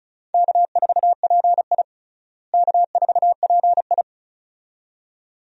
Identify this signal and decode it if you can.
Morse code, heard as a keyed tone, about 36 wpm, 700 Hz. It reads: K4PI K4PI
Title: USA Call Signs - 4 Characters - Repeated 2x - 35wpm